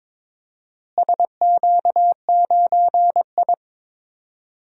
Morse code, heard as a keyed tone, about 22 wpm, 700 Hz. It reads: SQ9I
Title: DXCC-100 Call Signs - 4 Characters - Repeated 3x - 22wpm